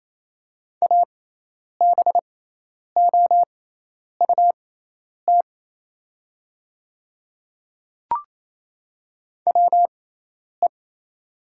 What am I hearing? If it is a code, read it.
Morse code, heard as a keyed tone, about 28 wpm, spaced as for 10 wpm, 700 Hz. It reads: ABOUT WE
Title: 6x Character Spacing - Top 100 Words - 28wpm